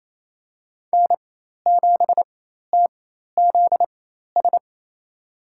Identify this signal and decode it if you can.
Morse code, heard as a keyed tone, about 28 wpm, spaced as for 13 wpm, 700 Hz. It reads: N7TZS
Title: USA Call Signs - 5 Characters - 4x Character Spacing - 28wpm